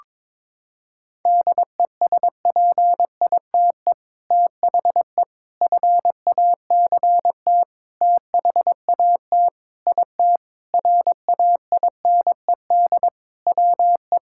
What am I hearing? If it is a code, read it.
Morse code, heard as a keyed tone, about 22 wpm, 700 Hz. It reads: DESPITE THE FACT THAT IT RAINED WE
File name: practice / 22wpm / Sentences with 4-Word Phrases - 22wpm